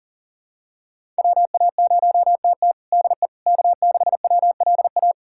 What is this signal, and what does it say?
Morse code, heard as a keyed tone, about 40 wpm, 700 Hz. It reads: WA0TT DE K6WLA
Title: USA Callsign DE Callsign - 5 Characters - 40wpm